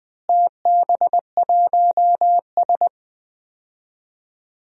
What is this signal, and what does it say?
Morse code, heard as a keyed tone, about 20 wpm, 700 Hz. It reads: TB1S
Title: DXCC-100 Call Signs - 4 Characters - Repeated 3x - 20wpm